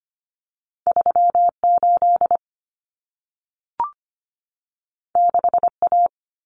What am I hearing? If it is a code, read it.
Morse code, heard as a keyed tone, about 25 wpm, 700 Hz. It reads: 38 6A
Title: Common Cut Numbers - 2-Digit 25wpm